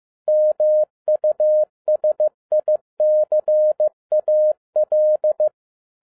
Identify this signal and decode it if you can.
Morse code, heard as a keyed tone, about 15 wpm, 610 Hz. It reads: MUSICAL